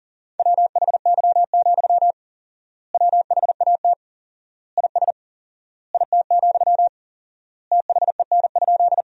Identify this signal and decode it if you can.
Morse code, heard as a keyed tone, about 40 wpm, 700 Hz. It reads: WHY, WHAT IS IT, THEN?
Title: 4x Word-Spacing - Sentences from Top 200 Words - 40wpm